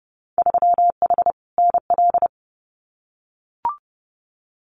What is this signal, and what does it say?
Morse code, heard as a keyed tone, about 31 wpm, 700 Hz. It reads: 3H NL
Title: Winter Field Day Exchanges Encore 30wpm